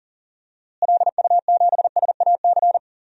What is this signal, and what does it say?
Morse code, heard as a keyed tone, about 37 wpm, 700 Hz. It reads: LU7SAC
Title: DXCC-100 Call Signs - 6 Characters - Repeated 3x - 40wpm